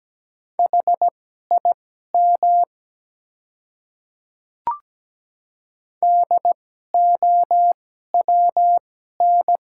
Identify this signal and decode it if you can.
Morse code, heard as a keyed tone, about 17 wpm, spaced as for 12 wpm, 700 Hz. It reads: HIM DOWN
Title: 2x Character Spacing - Top 100 Words - 17wpm